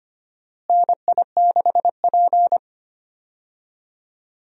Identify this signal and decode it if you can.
Morse code, heard as a keyed tone, about 25 wpm, 700 Hz. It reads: NI6P
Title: USA Call Signs - 4 Characters - Repeated 2x - 25wpm